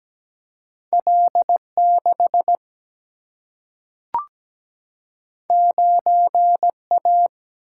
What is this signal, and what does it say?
Morse code, heard as a keyed tone, about 17 wpm, 700 Hz. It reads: L6 9A
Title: Single-Letter Single-Number 17wpm